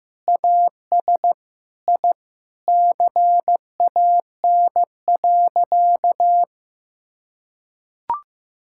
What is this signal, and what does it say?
Morse code, heard as a keyed tone, about 15 wpm, 700 Hz. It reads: AS I CAN.
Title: Sentences from Top 1000 Words - Review Part 5 - 15wpm